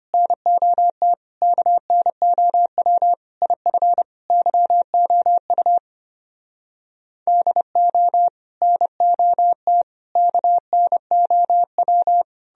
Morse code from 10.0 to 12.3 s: KNOW

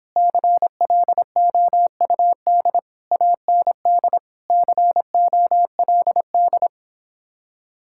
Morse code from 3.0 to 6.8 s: AND COLD

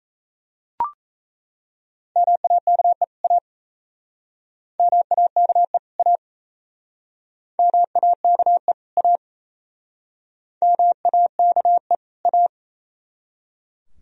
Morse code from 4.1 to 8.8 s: MAKE A MAKE

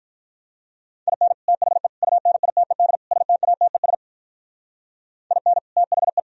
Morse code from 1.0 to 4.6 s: IN THE UNITED STATES

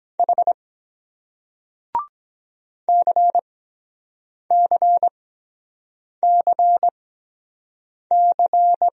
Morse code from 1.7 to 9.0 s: CCCC

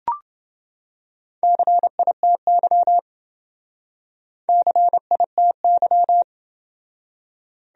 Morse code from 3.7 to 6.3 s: CITY

